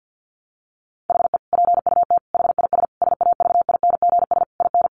Morse code, heard as a keyed tone, about 50 wpm, 700 Hz. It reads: HE PUT HIS SAVINGS IN
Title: Sentences from Top 1200 Words 50wpm